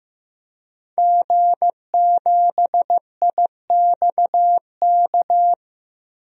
Morse code, heard as a keyed tone, about 15 wpm, 700 Hz. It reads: G7IXK
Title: DXCC-100 Call Signs - 5 Characters - 15wpm